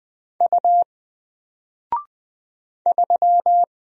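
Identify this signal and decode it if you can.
Morse code, heard as a keyed tone, about 20 wpm, 700 Hz. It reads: U 3
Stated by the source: Single letter and number Morse drill at 20 wpm, from N4LCD.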